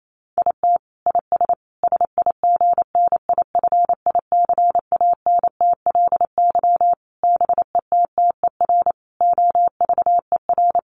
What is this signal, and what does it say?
Morse code, heard as a keyed tone, about 28 wpm, 700 Hz. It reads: IT IS SIGNIFICANTLY BETTER OVER
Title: Sentences from Top 1500 Words 28wpm